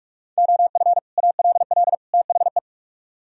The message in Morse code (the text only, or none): OF ALL THE